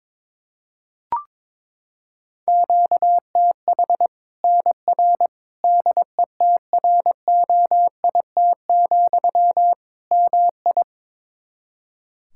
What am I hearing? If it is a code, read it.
Morse code, heard as a keyed tone, about 22 wpm, 700 Hz. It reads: QTH NR DETROIT, MI
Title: Practice QSO - Exchange 1 - Beginning - 22wpm